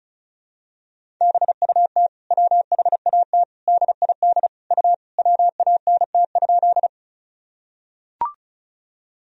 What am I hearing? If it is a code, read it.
Morse code, heard as a keyed tone, about 36 wpm, 700 Hz. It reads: BUT WHAT DID U WANT?
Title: Sentences from Top 200 Words and Abbreviations 35wpm